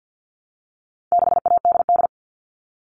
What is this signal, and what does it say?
Morse code, heard as a keyed tone, about 50 wpm, 700 Hz. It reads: 6ADD
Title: All Cut Numbers - 4-Digit 50wpm